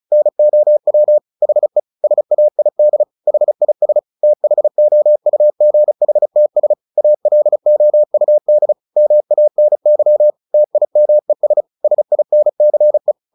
Morse code, heard as a keyed tone, about 34 wpm, 600 Hz. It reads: NOW HE SAID HIS THOUGHTS ALOUD MANY TIMES SINCE
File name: the old man and the sea / the old man and the sea 1 35wpm